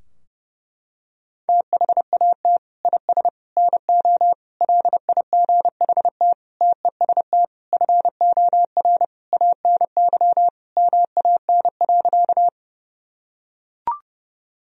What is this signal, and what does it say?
Morse code, heard as a keyed tone, about 30 wpm, 700 Hz. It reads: THAT IS NO LIGHT TEST FOR ANY MAN.
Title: Sentences from Top 1000 Words - Review Part 3 - 30wpm